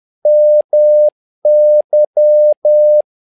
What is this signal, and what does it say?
Morse code, heard as a keyed tone, about 10 wpm, 600 Hz. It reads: MY
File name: learning center / tower trouble / tower trouble 10wpm